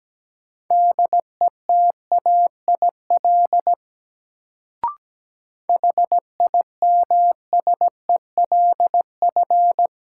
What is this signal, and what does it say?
Morse code, heard as a keyed tone, about 17 wpm, 700 Hz. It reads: DETAIL HIMSELF
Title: Sentences from Top 1500 Words - New Word Drill - 17wpm